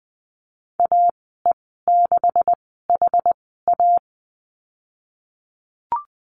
Morse code, heard as a keyed tone, about 20 wpm, spaced as for 15 wpm, 700 Hz. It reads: AE6HA
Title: 161 Lesson 26g - Letter g - Call Signs 20wpm